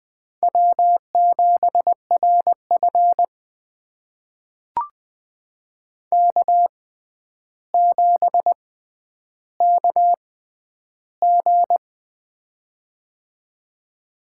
Morse code, heard as a keyed tone, about 20 wpm, spaced as for 10 wpm, 700 Hz. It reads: W7RF K7KG
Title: USA Call Signs - 6x Character Spacing - 20wpm